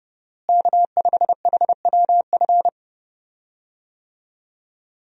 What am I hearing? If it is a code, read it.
Morse code, heard as a keyed tone, about 29 wpm, 700 Hz. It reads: K5HWF
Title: USA Call Signs - 5 Characters - Repeated 3x - 30wpm